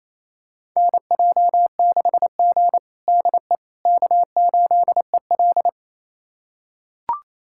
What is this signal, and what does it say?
Morse code, heard as a keyed tone, about 28 wpm, 700 Hz. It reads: NJ6G DE K8EL
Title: USA Callsign DE Callsign - 4 Characters - 28wpm